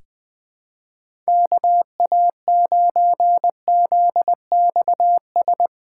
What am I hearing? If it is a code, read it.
Morse code, heard as a keyed tone, about 20 wpm, 700 Hz. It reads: KA9ZXS
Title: USA Callsign DE Callsign - 6 Characters - 20wpm